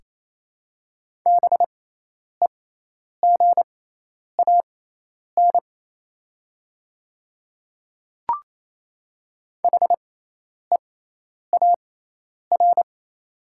Morse code, heard as a keyed tone, about 28 wpm, spaced as for 10 wpm, 700 Hz. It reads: BEGAN HEAR
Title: 6x Character Spacing - Top 300 Words - 28wpm